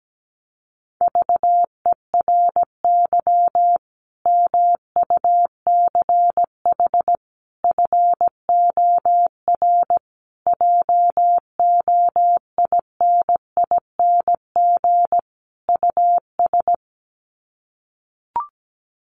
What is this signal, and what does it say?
Morse code, heard as a keyed tone, about 17 wpm, 700 Hz. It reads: VERY MUCH FOR JOINING US
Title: Sets of 5 Words 17wpm